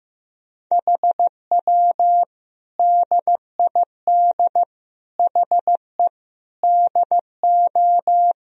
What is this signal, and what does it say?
Morse code, heard as a keyed tone, about 15 wpm, 700 Hz. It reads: HW DID HE DO
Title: Sentences from Top 100 Words and Abbreviations 15wpm